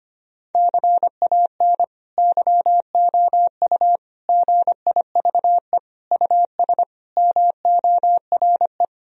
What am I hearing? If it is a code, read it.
Morse code, heard as a keyed tone, about 25 wpm, 700 Hz. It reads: CAN YOU GIVE US MORE
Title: Sentences from Top 1000 Words - Review Part 3 - 25wpm